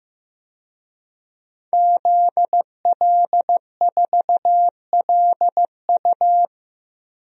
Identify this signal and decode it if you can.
Morse code, heard as a keyed tone, about 16 wpm, 700 Hz. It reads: ZL4LU
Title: DXCC-100 Call Signs - 5 Characters - Repeated 3x - 15wpm